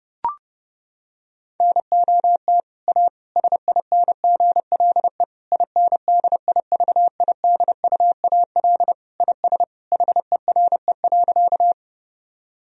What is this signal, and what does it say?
Morse code, heard as a keyed tone, about 30 wpm, 700 Hz. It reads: NOT A SINGLE INDIVIDUAL IS HERE.
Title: Sentences from Top 1000 Words - Review Part 4 - 30wpm